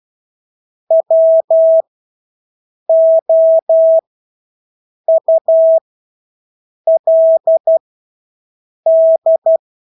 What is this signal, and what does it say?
Morse code, heard as a keyed tone, about 12 wpm, spaced as for 6 wpm, 650 Hz. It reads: WOULD
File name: Engelske ord 12-6 wpm